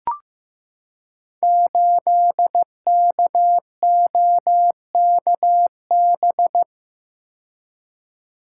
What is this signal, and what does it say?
Morse code, heard as a keyed tone, about 15 wpm, 700 Hz. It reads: 8KOKB
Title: Code Groups - 5 Letter:Numbers - 15wpm